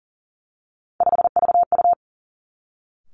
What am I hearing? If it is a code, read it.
Morse code, heard as a keyed tone, about 41 wpm, 700 Hz. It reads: 5VU
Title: Three Challenging Letter-Number Combinations 40wpm